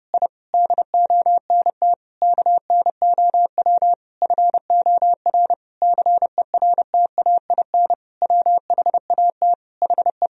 Morse code, heard as a keyed tone, about 32 wpm, 700 Hz. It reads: I DONT KNOW FOR CERTAIN WHAT HE
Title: Sentences from Top 1000 Words - Review Part 2 - 30wpm